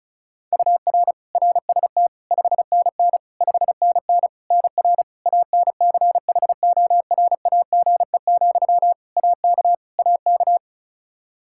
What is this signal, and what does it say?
Morse code, heard as a keyed tone, about 36 wpm, 700 Hz. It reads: UR RST 5NN 5NN NR ANCHORAGE, AK AK
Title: Practice QSO - Exchange 1 - Beginning - 35wpm